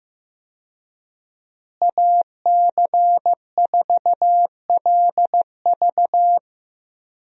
Morse code, heard as a keyed tone, about 15 wpm, 700 Hz. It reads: AC4LV
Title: USA Call Signs - 5 Characters - Repeated 2x - 15wpm